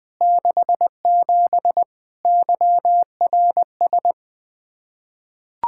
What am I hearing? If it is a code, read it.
Morse code, heard as a keyed tone, about 20 wpm, 700 Hz. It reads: 67 YRS
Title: Practice QSO - Exchange 3 - More Info - 20wpm